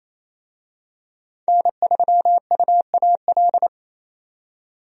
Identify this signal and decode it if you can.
Morse code, heard as a keyed tone, about 28 wpm, 700 Hz. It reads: N3UAL
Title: USA Call Signs - 5 Characters - 28wpm